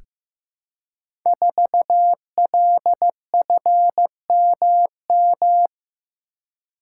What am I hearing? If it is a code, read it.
Morse code, heard as a keyed tone, about 15 wpm, 700 Hz. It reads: 4LFMM
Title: Code Groups - 5 Letter:Numbers - 15wpm